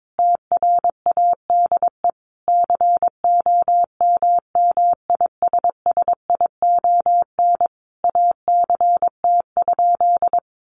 Morse code, heard as a keyed tone, about 22 wpm, 700 Hz. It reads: TRADE COMMISSION ACT?
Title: Sentences from Top 1400 Words 22wpm